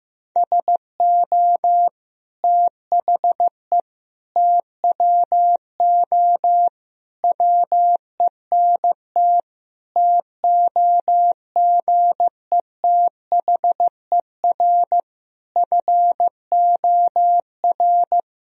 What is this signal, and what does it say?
Morse code, heard as a keyed tone, about 15 wpm, 700 Hz. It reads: SO THE TWO WENT TOGETHER FOR
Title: Sentences from Top 1000 Words - Review Part 1 - 15wpm